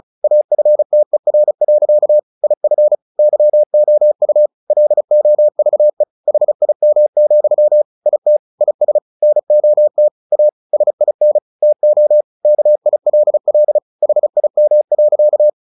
Morse code, heard as a keyed tone, about 34 wpm, 600 Hz. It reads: AFTER. IF YOU LOVE HIM, IT IS NOT A SIN TO KILL HIM.
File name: the old man and the sea 2 35wpm